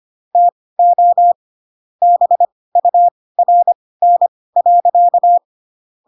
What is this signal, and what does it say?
Morse code, heard as a keyed tone, about 25 wpm, spaced as for 18 wpm, 700 Hz. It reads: TO BURN.
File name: Pinocchio / Pinocchio ch6 12 25 ik1hge